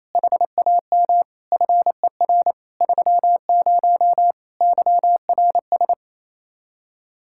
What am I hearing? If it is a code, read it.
Morse code, heard as a keyed tone, about 28 wpm, 700 Hz. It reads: HAM FER 30 YRS
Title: K1USN SST QSO Practice 28wpm